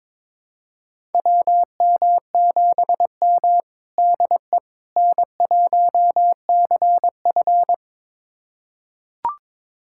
Morse code, heard as a keyed tone, about 21 wpm, 700 Hz. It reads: WM7M DE N1CF
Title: USA Callsign DE Callsign - 4 Characters - 22wpm